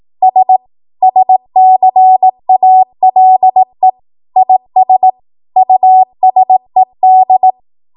Morse code, heard as a keyed tone, about 18 wpm, 750 Hz. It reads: S SCALE IS USED
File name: Morse / 18WPM